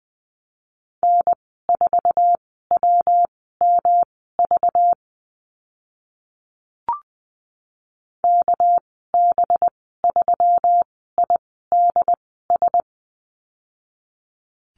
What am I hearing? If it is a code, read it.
Morse code, heard as a keyed tone, about 20 wpm, spaced as for 14 wpm, 700 Hz. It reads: N4WMV KB3IDS
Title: 207 Lesson 33g - Letter j - Call Signs 20wpm